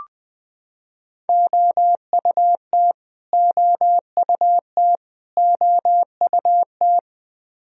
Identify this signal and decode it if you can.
Morse code, heard as a keyed tone, about 20 wpm, 700 Hz. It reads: OUT OUT OUT
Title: Top 100 Words Repeated 20wpm